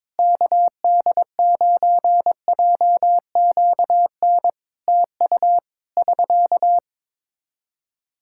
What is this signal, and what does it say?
Morse code, heard as a keyed tone, about 22 wpm, 700 Hz. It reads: KD9JQN TU <SK>
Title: Practice QSO - Exchange 4 - Ending - 22wpm